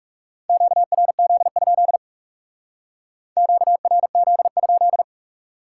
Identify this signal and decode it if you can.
Morse code, heard as a keyed tone, about 38 wpm, 700 Hz. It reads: QRZ? QRZ?